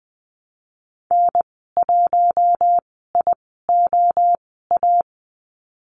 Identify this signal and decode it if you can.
Morse code, heard as a keyed tone, about 20 wpm, spaced as for 14 wpm, 700 Hz. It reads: N1IOA
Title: 069 Lesson 12g - Letter d - Call Signs 20wpm